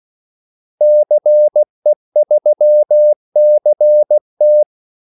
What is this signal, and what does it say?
Morse code, heard as a keyed tone, about 16 wpm, 600 Hz. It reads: CE3CT
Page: Kaldesignaler 16 wpm